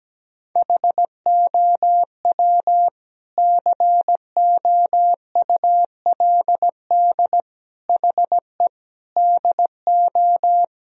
Morse code, heard as a keyed tone, about 17 wpm, 700 Hz. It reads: HOW COULD HE DO
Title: Sentences from Top 1000 Words - Review Part 1 - 17wpm